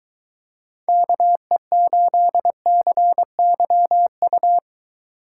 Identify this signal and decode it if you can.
Morse code, heard as a keyed tone, about 22 wpm, 700 Hz. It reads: KE8CYU